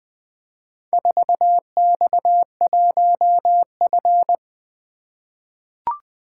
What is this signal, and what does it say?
Morse code, heard as a keyed tone, about 20 wpm, 700 Hz. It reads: 4X1F
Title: Four Letter-Number Combinations - 20wpm